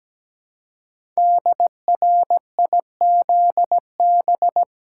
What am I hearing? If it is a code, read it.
Morse code, heard as a keyed tone, about 17 wpm, 700 Hz. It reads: DRIZB
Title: Code Groups - 5 Letters - 17wpm